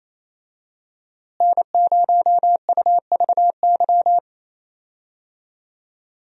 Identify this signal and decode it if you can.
Morse code, heard as a keyed tone, about 28 wpm, 700 Hz. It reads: N0UVY
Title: USA Call Signs - 5 Characters - Repeated 2x - 28wpm